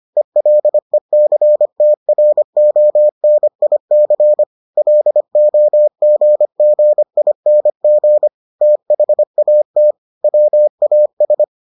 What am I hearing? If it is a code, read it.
Morse code, heard as a keyed tone, about 25 wpm, 600 Hz. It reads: ELECTRONIC LOGGING THAT WAS